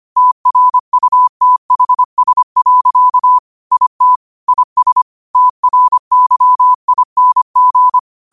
Morse code, heard as a keyed tone, about 25 wpm, 1000 Hz. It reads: TRUTHS. IT IS TRYING